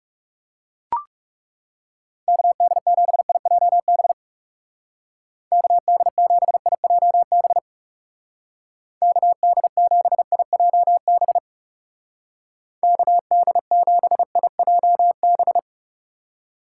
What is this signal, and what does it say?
Morse code, heard as a keyed tone, about 38 wpm, 700 Hz. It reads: KD7IJB KD7IJB KD7IJB KD7IJB